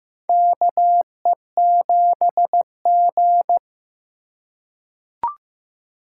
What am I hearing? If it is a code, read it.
Morse code, heard as a keyed tone, about 15 wpm, 700 Hz. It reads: KE7G
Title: USA Call Signs - 3x Character Spacing - 15wpm